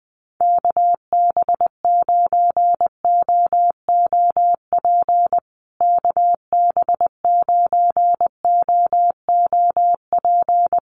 Morse code, heard as a keyed tone, about 20 wpm, 700 Hz. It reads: KB9OOP KB9OOP
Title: USA Call Signs Repeated 3x 20wpm